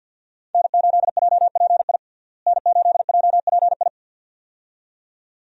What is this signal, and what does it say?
Morse code, heard as a keyed tone, about 48 wpm, 700 Hz. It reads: N8JPI N8JPI
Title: USA Call Signs - 5 Characters - Repeated 2x - 50wpm